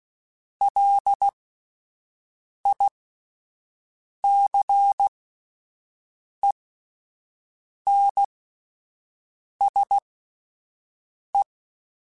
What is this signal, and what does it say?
Morse code, heard as a keyed tone, about 16 wpm, spaced as for 5 wpm, 800 Hz. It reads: LICENSE
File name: (spaced) / QSOs Spaced-mp3s / WA7QOX 2 @10 sp